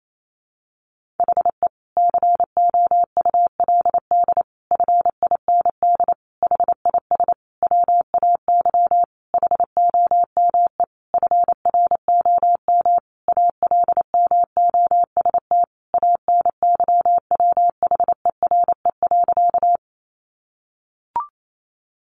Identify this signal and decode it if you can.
Morse code, heard as a keyed tone, about 28 wpm, 700 Hz. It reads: HE COULD FIND HIS WAY HOME FROM ALMOST ANYWHERE.
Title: Sentences from Top 1400 Words 28wpm